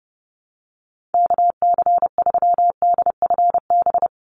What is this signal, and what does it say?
Morse code, heard as a keyed tone, about 28 wpm, 700 Hz. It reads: KC3DFB
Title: USA Call Signs 30wpm